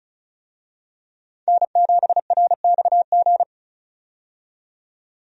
Morse code, heard as a keyed tone, about 36 wpm, 700 Hz. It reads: N7RXG
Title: USA Call Signs - 5 Characters - 35wpm